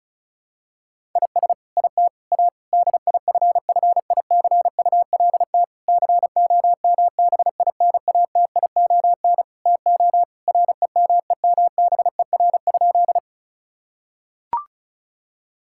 Morse code, heard as a keyed tone, about 36 wpm, 700 Hz. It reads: IS IT A DIFFICULT COMBINATION TO REMEMBER?
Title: Sentences from Top 1000 Words - Review Part 6 - 35wpm